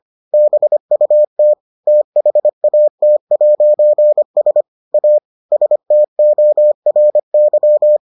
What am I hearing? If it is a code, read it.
Morse code, heard as a keyed tone, about 25 wpm, 600 Hz. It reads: BUT THAT'S A STORY